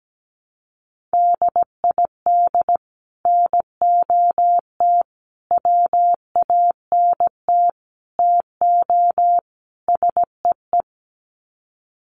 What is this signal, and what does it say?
Morse code, heard as a keyed tone, about 17 wpm, 700 Hz. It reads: DID NOT WANT TO SEE
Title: Sets of 5 Words Encore2 17wpm